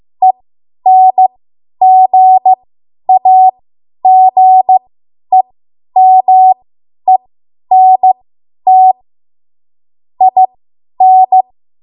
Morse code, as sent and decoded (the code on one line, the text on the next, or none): . -. --. .- --. -- . -. - / .. -.
ENGAGEMENT IN